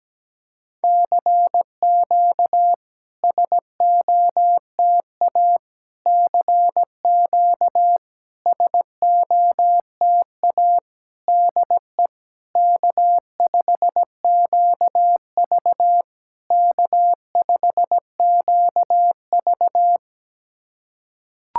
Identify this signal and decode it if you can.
Morse code, heard as a keyed tone, about 17 wpm, 690 Hz. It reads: CQ SOTA CQ SOTA DE K5QV K5QV
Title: SOTA QSO Practice 17wpm